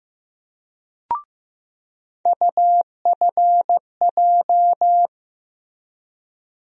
Morse code, heard as a keyed tone, about 15 wpm, 700 Hz. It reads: UFJ
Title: USA Call Sign Postfixes - 3 Letters 15wpm